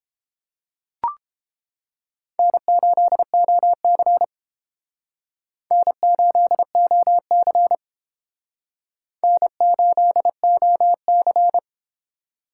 Morse code, heard as a keyed tone, about 29 wpm, 700 Hz. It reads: N8OC N8OC N8OC